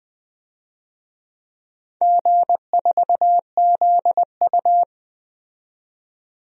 Morse code, heard as a keyed tone, about 20 wpm, 700 Hz. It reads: G4ZU
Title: DXCC-100 Call Signs - 4 Characters - 20wpm